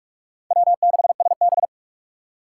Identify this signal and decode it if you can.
Morse code, heard as a keyed tone, about 46 wpm, 700 Hz. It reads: W6SB
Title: USA Call Signs - 4 Characters - 3x Character Spacing - 45wpm